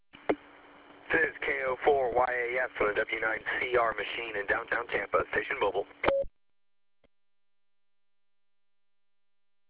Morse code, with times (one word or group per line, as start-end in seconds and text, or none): none